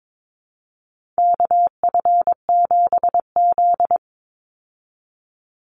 Morse code, heard as a keyed tone, about 22 wpm, 700 Hz. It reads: KF7Z
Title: Call Signs Repeated 3x - 22wpm